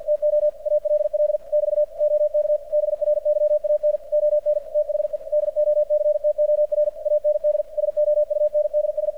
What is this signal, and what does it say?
Morse code, heard as a keyed tone, about 53 wpm, 600 Hz, with radio noise. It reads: TO ALL <BT> OK BRYAN ON THE DOCTOR AND SOUNDS
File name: archived / CRSnet